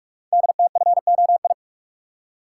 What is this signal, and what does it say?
Morse code, heard as a keyed tone, about 44 wpm, 690 Hz. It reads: DTFOI
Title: Code Groups - 5 Letters - 45wpm